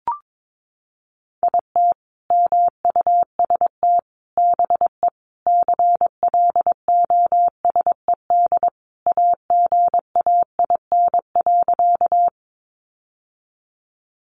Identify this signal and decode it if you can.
Morse code, heard as a keyed tone, about 22 wpm, 700 Hz. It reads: IT MUST BE CLOSED AGAIN.
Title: Sentences from Top 1400 Words 22wpm